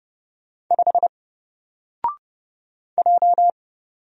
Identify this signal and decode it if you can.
Morse code, heard as a keyed tone, about 30 wpm, 700 Hz. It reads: 5 J
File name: Single-Letter-Number-30wpm